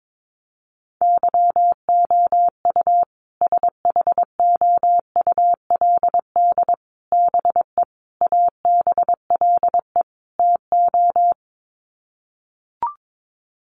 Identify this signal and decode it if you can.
Morse code, heard as a keyed tone, about 22 wpm, 700 Hz. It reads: YOU SHOULD BE ABLE TO
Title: Sets of 5 Words Encore 22wpm